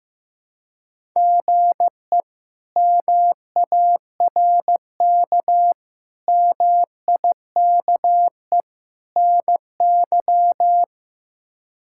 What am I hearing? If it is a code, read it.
Morse code, heard as a keyed tone, about 15 wpm, 700 Hz. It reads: GE MARK MIKE NY
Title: K1USN SST QSO Practice - 15wpm